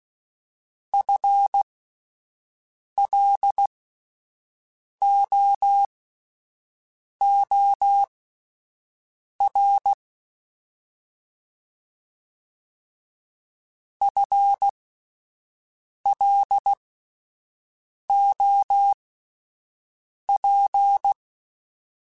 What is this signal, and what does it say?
Morse code, heard as a keyed tone, about 16 wpm, spaced as for 5 wpm, 800 Hz. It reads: FLOOR FLOP
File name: Spaced / FL Runs sp mp3s / FL Runs sp @10